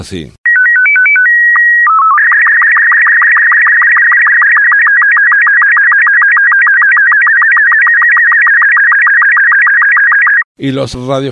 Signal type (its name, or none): SSTV (slow-scan television)